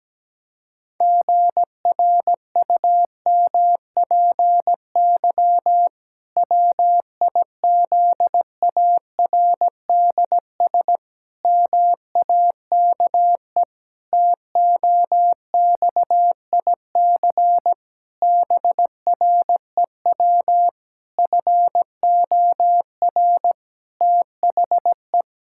Morse code, heard as a keyed tone, about 17 wpm, 700 Hz. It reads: GRUMPY WIZARDS MAKE TOXIC BREW FOR THE
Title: Pangram Challenge v2 - 17wpm